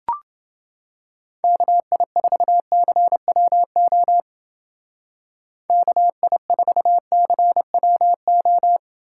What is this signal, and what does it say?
Morse code, heard as a keyed tone, about 28 wpm, 700 Hz. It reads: KI4CWO KI4CWO